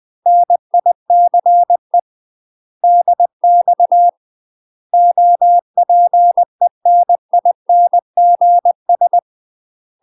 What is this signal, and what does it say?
Morse code, heard as a keyed tone, about 20 wpm, 700 Hz. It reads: NICE DX OPENINGS
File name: UK MASTER QSOs TRAINING / 20WPM / UK MASTER QSOs 0049